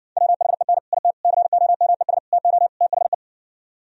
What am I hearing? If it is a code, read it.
Morse code, heard as a keyed tone, about 60 wpm, 700 Hz. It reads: WHEN IT COMES TO THE